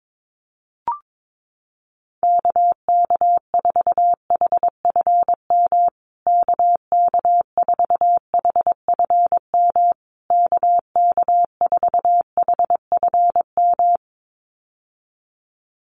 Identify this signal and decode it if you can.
Morse code, heard as a keyed tone, about 22 wpm, 700 Hz. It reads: KK4HFM KK4HFM KK4HFM
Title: USA Call Signs Repeated 3x 22wpm